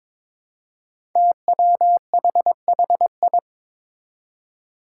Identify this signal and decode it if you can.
Morse code, heard as a keyed tone, about 22 wpm, 700 Hz. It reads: TWHHI
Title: Five Challenging Letter-Number Combinations 22wpm